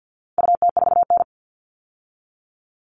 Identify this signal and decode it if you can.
Morse code, heard as a keyed tone, about 50 wpm, 700 Hz. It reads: UT4N
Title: All Cut Numbers - 4-Digit 50wpm